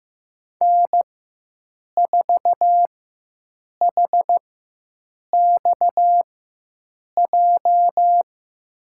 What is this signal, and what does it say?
Morse code, heard as a keyed tone, about 15 wpm, spaced as for 7 wpm, 700 Hz. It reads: N4HXJ
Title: USA Call Signs - 5 Characters - 4x Character Spacing - 15wpm